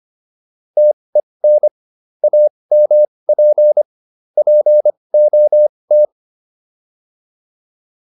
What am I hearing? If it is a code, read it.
Morse code, heard as a keyed tone, about 25 wpm, spaced as for 20 wpm, 600 Hz. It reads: TEN AMP POT